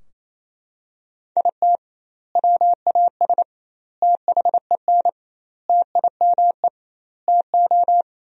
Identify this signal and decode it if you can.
Morse code, heard as a keyed tone, about 28 wpm, 700 Hz. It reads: IT WAS THEN TIME TO
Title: 2x Word-Spacing - Sentences from Top 300 Words - 28wpm